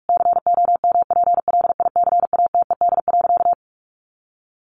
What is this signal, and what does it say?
Morse code, heard as a keyed tone, about 44 wpm, 700 Hz. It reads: COMPLICATED.
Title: Sentences from Top 1200 Words 45wpm